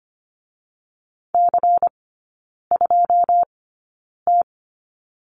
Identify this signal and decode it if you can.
Morse code, heard as a keyed tone, about 25 wpm, spaced as for 9 wpm, 700 Hz. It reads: C2T
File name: Lesson 17e - Letter u - Three Characters 25wpm